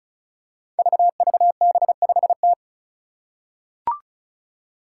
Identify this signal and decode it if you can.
Morse code, heard as a keyed tone, about 34 wpm, 700 Hz. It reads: VVB5T
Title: Five Challenging Letter-Number Combinations 35wpm